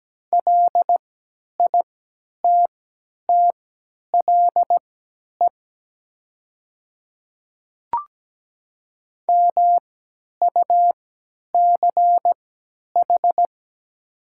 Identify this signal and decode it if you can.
Morse code, heard as a keyed tone, about 17 wpm, spaced as for 10 wpm, 700 Hz. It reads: LITTLE MUCH
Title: 3x Character Spacing - Top 200 Words - 17wpm